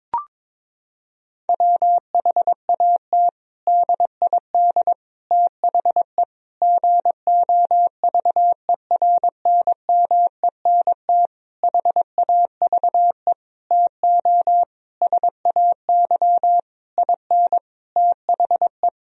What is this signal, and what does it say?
Morse code, heard as a keyed tone, about 22 wpm, 700 Hz. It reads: WHAT DID THE GOVERNMENT HAVE TO SAY IN THE
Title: Sentences from Top 1000 Words - Review Part 3 - 22wpm